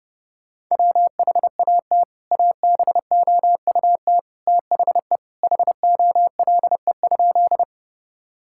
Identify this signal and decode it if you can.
Morse code, heard as a keyed tone, about 30 wpm, 700 Hz. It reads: WHAT ABOUT THE HOLE?
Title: Sentences from Top 1600 Words 30wpm